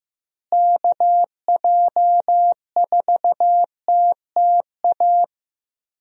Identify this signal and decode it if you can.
Morse code, heard as a keyed tone, about 16 wpm, 700 Hz. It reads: KJ4TTA
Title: USA Call Signs - 6 Characters - Repeated 3x - 15wpm